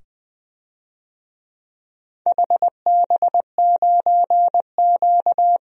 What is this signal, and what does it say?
Morse code, heard as a keyed tone, about 20 wpm, 700 Hz. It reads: HB9Q
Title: DXCC-100 Call Signs - 4 Characters - 20wpm